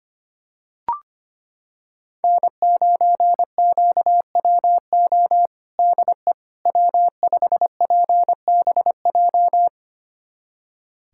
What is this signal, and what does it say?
Morse code, heard as a keyed tone, about 25 wpm, 700 Hz. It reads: N9QWO DE W5PBJ
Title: USA Callsign DE Callsign - 5 Characters - 25wpm